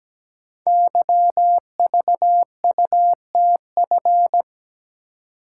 Morse code, heard as a keyed tone, about 17 wpm, 690 Hz. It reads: YVUTF
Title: Code Groups - 5 Letter:Numbers - 17wpm